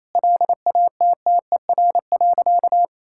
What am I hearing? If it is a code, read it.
Morse code, heard as a keyed tone, about 27 wpm, 700 Hz. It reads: LATTER.
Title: Sentences from Top 1600 Words 28wpm